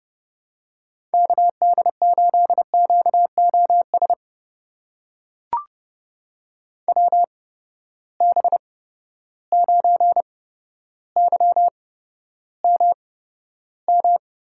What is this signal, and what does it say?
Morse code, heard as a keyed tone, about 30 wpm, spaced as for 13 wpm, 700 Hz. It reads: KD8QOS WB9YMM
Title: USA Call Signs - 6 Characters - 8x Character Spacing - 30wpm